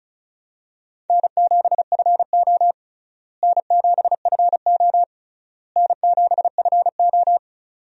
Morse code, heard as a keyed tone, about 36 wpm, 700 Hz. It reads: N7FO N7FO N7FO
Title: USA Call Signs - 4 Characters - Repeated 3x - 35wpm